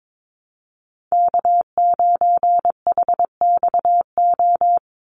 Morse code, heard as a keyed tone, about 22 wpm, 700 Hz. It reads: K9HXO
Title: Call Signs Repeated 2x - 22wpm